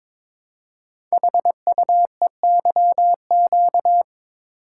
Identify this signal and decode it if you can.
Morse code, heard as a keyed tone, about 22 wpm, 700 Hz. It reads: HUEYQ
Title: Code Groups - 5 Letters - 22wpm